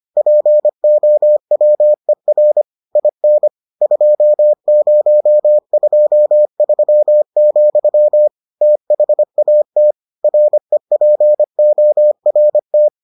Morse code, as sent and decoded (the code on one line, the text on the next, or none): .--. --- .-- . .-. / .. -. / ..--- ----- ..--- ...-- --..-- / - .... .- - / .-. .--. --- .-. -
POWER IN 2023, THAT REPORT